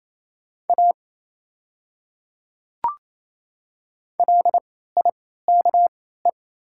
Morse code, heard as a keyed tone, about 28 wpm, spaced as for 16 wpm, 690 Hz. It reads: A LIKE
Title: 3x Character Spacing - Top 100 Words - 28wpm